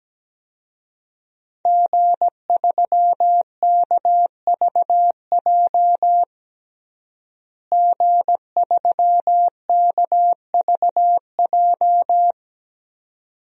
Morse code, heard as a keyed tone, about 17 wpm, 700 Hz. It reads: G3KVJ G3KVJ
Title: DXCC-100 Call Signs - 5 Characters - Repeated 2x - 17wpm